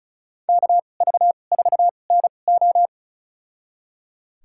Morse code, heard as a keyed tone, about 36 wpm, spaced as for 25 wpm, 700 Hz. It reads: KV4NO
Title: USA Call Signs - 5 Characters - 2x Character Spacing - 35wpm